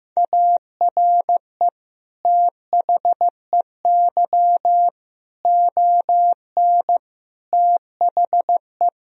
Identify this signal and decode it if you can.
Morse code, heard as a keyed tone, about 15 wpm, 700 Hz. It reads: ARE THEY ON THE
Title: Sentences from Top 1000 Words - Review Part 2 - 15wpm